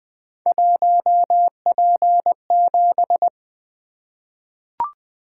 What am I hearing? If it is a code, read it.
Morse code, heard as a keyed tone, about 20 wpm, 700 Hz. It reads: 1P7
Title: Three Letter-Number Combinations - 20wpm